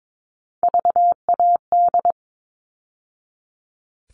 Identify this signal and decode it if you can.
Morse code, heard as a keyed tone, about 22 wpm, 700 Hz. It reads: VAD
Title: All Cut Numbers - 3-Digit 22wpm